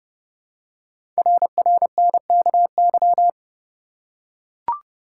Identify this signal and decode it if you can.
Morse code, heard as a keyed tone, about 30 wpm, 690 Hz. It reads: RRNKY
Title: Code Groups - 5 Letters - 30wpm